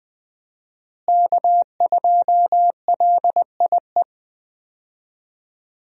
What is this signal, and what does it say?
Morse code, heard as a keyed tone, about 20 wpm, 700 Hz. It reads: K2LIE
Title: USA Call Signs - 5 Characters - Repeated 2x - 20wpm